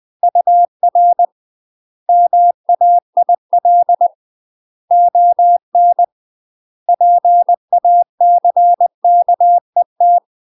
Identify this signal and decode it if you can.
Morse code, heard as a keyed tone, about 20 wpm, 700 Hz. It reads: UR MAIL ON PACKET
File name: UK MASTER QSOs TRAINING / 20WPM / UK MASTER QSOs 0069